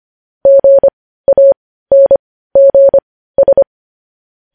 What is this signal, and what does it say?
Morse code, heard as a keyed tone, about 25 wpm, spaced as for 15 wpm, 550 Hz. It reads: GANGS
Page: news-headlines-in-morse-code-at-15-wpm